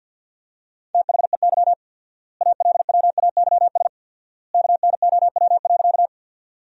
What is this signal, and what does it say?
Morse code, heard as a keyed tone, about 50 wpm, 700 Hz. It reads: THEY ALWAYS KNOW.
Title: 4x Word-Spacing - Sentences from Top 300 Words - 50wpm